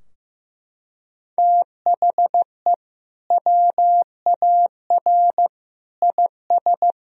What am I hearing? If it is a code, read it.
Morse code, heard as a keyed tone, about 15 wpm, 700 Hz. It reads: THE WAR IS
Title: Sentences from Top 1000 Words - Review Part 4 - 15wpm